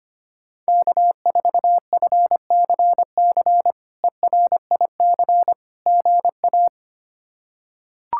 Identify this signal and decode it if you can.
Morse code, heard as a keyed tone, about 25 wpm, 700 Hz. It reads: K4FCC ERIC GA
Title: K1USN SST QSO Practice - 25wpm